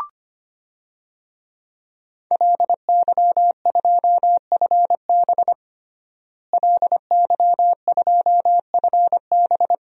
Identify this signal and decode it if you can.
Morse code, heard as a keyed tone, about 25 wpm, 700 Hz. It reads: LY2FB LY2FB
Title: DXCC-100 Call Signs - 5 Characters - Repeated 2x - 25wpm